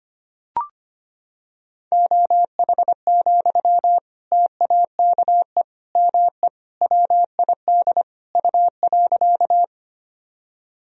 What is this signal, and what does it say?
Morse code, heard as a keyed tone, about 25 wpm, 700 Hz. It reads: OH, TAKE ME WID U.
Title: Sentences from Top 500 Words and Abbreviations 25wpm